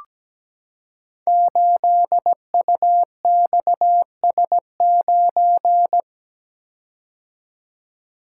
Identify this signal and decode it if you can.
Morse code, heard as a keyed tone, about 17 wpm, 700 Hz. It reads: 8UXS9
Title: Code Groups - 5 Letter:Numbers - 17wpm